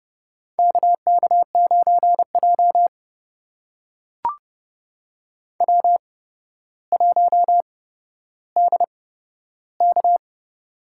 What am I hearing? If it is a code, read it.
Morse code, heard as a keyed tone, about 30 wpm, spaced as for 13 wpm, 700 Hz. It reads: KK9J W1DK
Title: USA Call Signs - 4 Characters - 8x Character Spacing - 30wpm